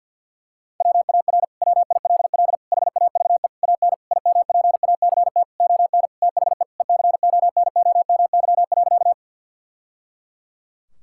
Morse code, heard as a keyed tone, about 51 wpm, 700 Hz. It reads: WAR WILL HAVE AN IMPACT ON THE ECONOMY.
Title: Sentences from Top 1000 Words - Review Part 6 - 50wpm